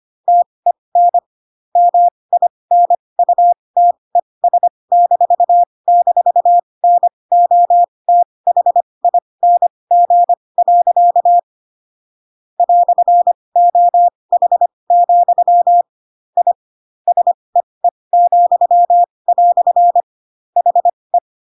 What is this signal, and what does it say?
Morse code, heard as a keyed tone, about 25 wpm, spaced as for 20 wpm, 700 Hz. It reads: TEN MINUTES--NOTHING. "OH, I SEE," HE
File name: Pinocchio / Pinocchio ch1 15 25 ik1hge